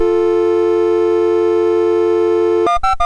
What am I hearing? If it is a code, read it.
Dial tone (North America), 350 + 440 Hz, with noise.
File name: Misc / SoundEffects / Dial Tone T001.mus MOS6581R3